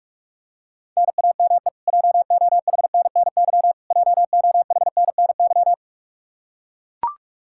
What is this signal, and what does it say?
Morse code, heard as a keyed tone, about 42 wpm, 700 Hz. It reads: NAME JOHNNY JOHNNY
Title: Practice QSO - Exchange 1 - Beginning - 45wpm